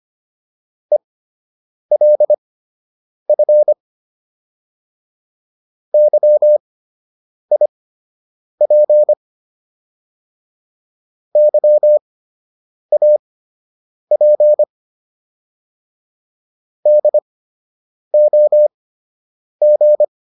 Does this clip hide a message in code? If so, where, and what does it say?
Yes: Morse code from 0.9 to 20.0 s, heard as a keyed tone, about 25 wpm, spaced as for 8 wpm, 600 Hz. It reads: ELF YIP YAP DOG